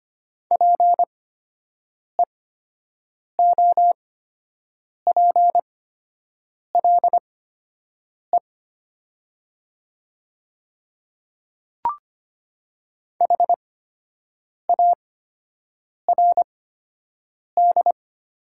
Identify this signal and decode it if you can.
Morse code, heard as a keyed tone, about 25 wpm, spaced as for 7 wpm, 700 Hz. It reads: PEOPLE HARD
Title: 8x Character Spacing - Top 300 Words - 25wpm